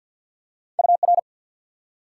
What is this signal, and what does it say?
Morse code, heard as a keyed tone, about 48 wpm, 700 Hz. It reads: UR
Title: Top 90 QSO Elements 50wpm